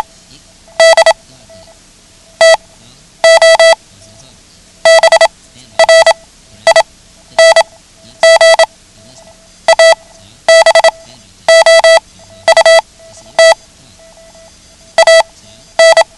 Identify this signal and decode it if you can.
Morse code, heard as a keyed tone, about 27 wpm, spaced as for 12 wpm, 660 Hz. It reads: D TO BRING ABOUT AN
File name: ham radio / CodeQuick / Code Quick 09